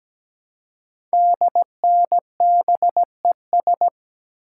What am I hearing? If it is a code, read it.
Morse code, heard as a keyed tone, about 17 wpm, 700 Hz. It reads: DNBES
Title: Code Groups - 5 Letters - 17wpm